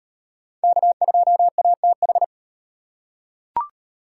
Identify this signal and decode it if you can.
Morse code, heard as a keyed tone, about 33 wpm, 700 Hz. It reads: K2ATH